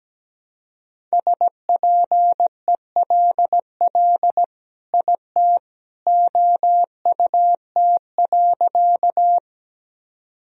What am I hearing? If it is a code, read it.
Morse code, heard as a keyed tone, about 17 wpm, 700 Hz. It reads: SPELL IT OUT.
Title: Sentences from Top 1000 Words - Review Part 1 - 17wpm